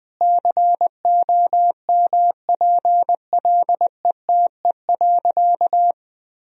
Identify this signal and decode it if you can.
Morse code, heard as a keyed tone, about 21 wpm, 700 Hz. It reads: COMPLETE.
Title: Sentences from Top 400 Words and Abbreviations 20wpm